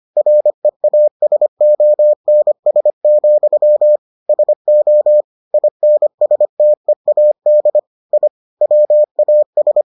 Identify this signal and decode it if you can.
Morse code, heard as a keyed tone, about 25 wpm, 600 Hz. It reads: REASONS, SO INSTEAD I WAS